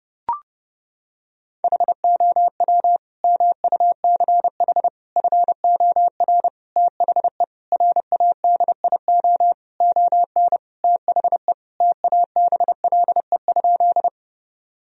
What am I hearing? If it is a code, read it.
Morse code, heard as a keyed tone, about 32 wpm, 700 Hz. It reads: HOW MUCH FOR THE RADIO ON THE TABLE?
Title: Sentences from Top 1000 Words - Review Part 3 - 30wpm